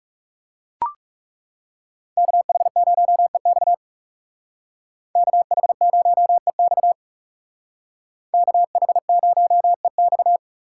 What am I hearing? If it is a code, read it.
Morse code, heard as a keyed tone, about 36 wpm, 700 Hz. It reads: KH0EX KH0EX KH0EX
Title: Speed Racing - USA 2x2 Call Signs 30wpm